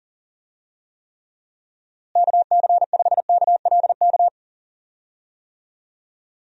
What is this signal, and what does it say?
Morse code, heard as a keyed tone, about 37 wpm, 700 Hz. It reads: KC5KLK
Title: USA Call Signs - 6 Characters - Repeated 2x - 40wpm